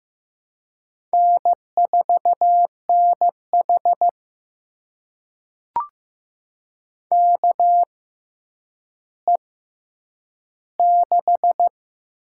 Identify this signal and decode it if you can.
Morse code, heard as a keyed tone, about 15 wpm, spaced as for 9 wpm, 700 Hz. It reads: N4NH KE6
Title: USA Call Signs - 6x Character Spacing - 15wpm